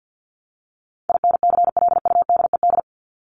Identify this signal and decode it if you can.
Morse code, heard as a keyed tone, about 50 wpm, 700 Hz. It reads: INCLUDED
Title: ED Suffix - Top 100 Words with Base Words 50wpm